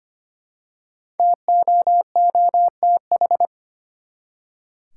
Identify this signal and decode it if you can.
Morse code, heard as a keyed tone, about 25 wpm, 700 Hz. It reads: TOOTH
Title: Five Letter Words - 25wpm